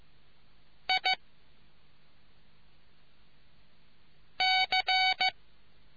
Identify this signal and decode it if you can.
Morse code, heard as a keyed tone, about 15 wpm, spaced as for 3 wpm, 750 Hz. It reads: IC